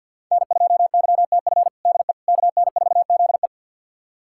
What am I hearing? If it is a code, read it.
Morse code, heard as a keyed tone, about 50 wpm, 690 Hz. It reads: N2YTF DE KN4ZE
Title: USA Callsign DE Callsign - 5 Characters - 50wpm